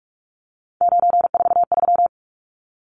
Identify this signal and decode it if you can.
Morse code, heard as a keyed tone, about 44 wpm, 700 Hz. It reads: 943
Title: Numbers - 3-Digit 45wpm